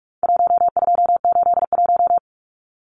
Morse code, heard as a keyed tone, about 44 wpm, 700 Hz. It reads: 1281
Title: Numbers - 4-Digit 45wpm